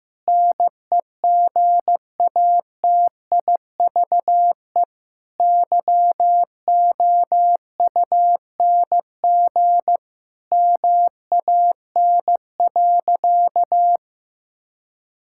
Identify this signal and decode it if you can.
Morse code, heard as a keyed tone, about 15 wpm, 700 Hz. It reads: NEGATIVE YOUNG MAN.
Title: Sentences from Top 1000 Words - Review Part 6 - 15wpm